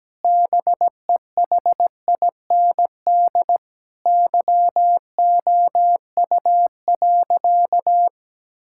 Morse code, heard as a keyed tone, about 17 wpm, 700 Hz. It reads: BEHIND YOU.
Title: Sentences from Top 1000 Words - Review Part 6 - 17wpm